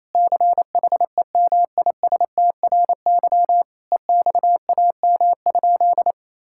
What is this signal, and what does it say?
Morse code, heard as a keyed tone, about 28 wpm, 700 Hz. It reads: CHEMISTRY EXAM?
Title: Sentences from Top 1000 Words - Review Part 6 - 28wpm